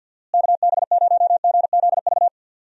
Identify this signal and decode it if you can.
Morse code, heard as a keyed tone, about 52 wpm, 700 Hz. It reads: KB0GZV
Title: USA Call Signs - 6 Characters - Repeated 3x - 50wpm